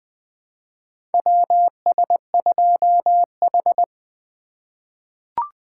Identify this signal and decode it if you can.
Morse code, heard as a keyed tone, about 20 wpm, 700 Hz. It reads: WS2H
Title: USA Call Signs - 4 Characters - 2x Character Spacing - 20wpm